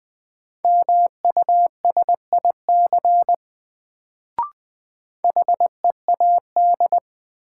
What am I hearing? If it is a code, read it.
Morse code, heard as a keyed tone, about 20 wpm, 700 Hz. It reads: MUSIC HEAD
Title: Sentences from Top 300 Words - New Word Drill - 20wpm